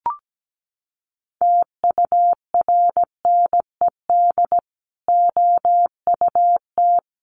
Morse code, heard as a keyed tone, about 17 wpm, 700 Hz. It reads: TURNED OUT
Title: Sets of 5 Words Encore 17wpm